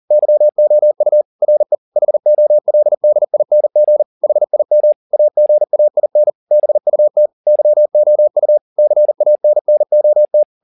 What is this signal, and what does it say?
Morse code, heard as a keyed tone, about 40 wpm, 600 Hz. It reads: YOU RE HOLDING HIM AGAIN BUT YOU CANNOT
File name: the old man and the sea 2 40wpm